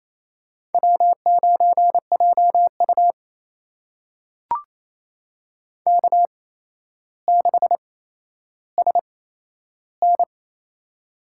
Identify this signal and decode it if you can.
Morse code, heard as a keyed tone, about 28 wpm, spaced as for 12 wpm, 700 Hz. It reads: W9JU K6SN
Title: USA Call Signs - 4 Characters - 8x Character Spacing - 28wpm